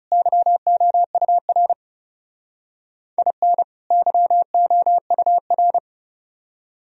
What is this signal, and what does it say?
Morse code, heard as a keyed tone, about 39 wpm, 700 Hz. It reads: YOUR IN YOUR